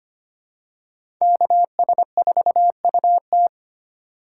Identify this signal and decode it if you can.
Morse code, heard as a keyed tone, about 25 wpm, 700 Hz. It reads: KS4UT
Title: USA Call Signs - 2x Character Spacing - 25wpm